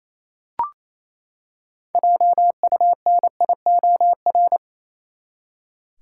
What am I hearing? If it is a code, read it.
Morse code, heard as a keyed tone, about 27 wpm, 700 Hz. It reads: JUNIOR
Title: Six Letter Words - 28wpm